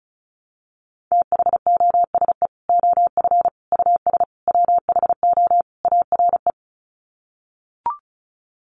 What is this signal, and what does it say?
Morse code, heard as a keyed tone, about 34 wpm, 700 Hz. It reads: THOSE OF US WHO ARE
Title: Sets of 5 Words Encore2 35wpm